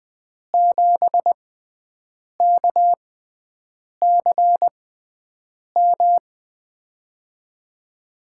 Morse code, heard as a keyed tone, about 20 wpm, spaced as for 7 wpm, 700 Hz. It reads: 7KCM
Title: USA Call Signs - 6 Characters - 6x Character Spacing - 20wpm